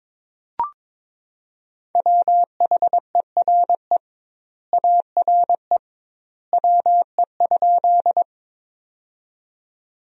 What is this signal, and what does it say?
Morse code, heard as a keyed tone, about 22 wpm, 700 Hz. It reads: WHERE ARE WE?
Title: 2x Word-Spacing - Sentences from Top 200 Words - 22wpm